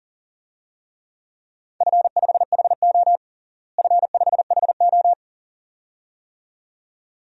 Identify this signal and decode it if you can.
Morse code, heard as a keyed tone, about 40 wpm, 700 Hz. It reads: F5HO F5HO
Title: DXCC-100 Call Signs - 4 Characters - Repeated 2x - 40wpm